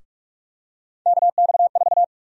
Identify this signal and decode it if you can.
Morse code, heard as a keyed tone, about 46 wpm, 700 Hz. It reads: KX4
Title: USA Call Sign Prefixes - 45wpm